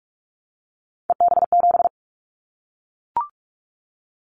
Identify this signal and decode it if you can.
Morse code, heard as a keyed tone, about 44 wpm, 700 Hz. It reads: EB7
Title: Callsign Prefixes for Top 25 DXCC - Repeated 3x - 45wpm